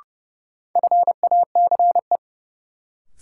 Morse code, heard as a keyed tone, about 32 wpm, 700 Hz. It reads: FACE
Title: Sets of 3 Words - New Word Drill - 30wpm